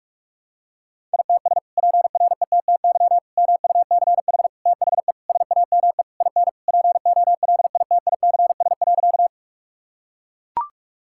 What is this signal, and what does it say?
Morse code, heard as a keyed tone, about 42 wpm, 700 Hz. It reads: ITS PRETTY MUCH THE SAME IN POLITICS.
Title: Sentences from Top 1000 Words - Review Part 6 - 45wpm